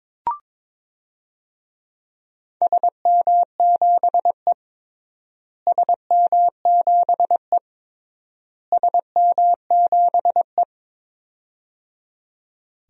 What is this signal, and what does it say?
Morse code, heard as a keyed tone, about 22 wpm, 700 Hz. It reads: SM7E SM7E SM7E
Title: DXCC-100 Call Signs - 4 Characters - Repeated 3x - 22wpm